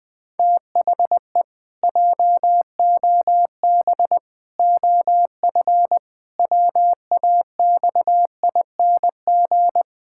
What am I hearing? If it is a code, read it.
Morse code, heard as a keyed tone, about 21 wpm, 700 Hz. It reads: THE JOB OF WAXING
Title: Pangram Challenge v2 - 20wpm